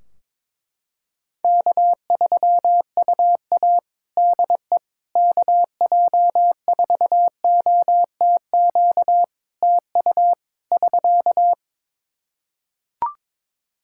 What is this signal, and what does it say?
Morse code, heard as a keyed tone, about 22 wpm, 700 Hz. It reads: K3UA DE KJ4OTQ TU <SK>
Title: Practice QSO - Exchange 4 - Ending - 22wpm